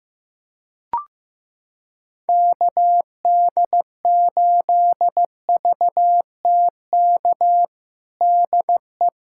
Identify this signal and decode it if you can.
Morse code, heard as a keyed tone, about 15 wpm, 700 Hz. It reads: KD8VTK DE
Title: USA Callsign DE Callsign 15wpm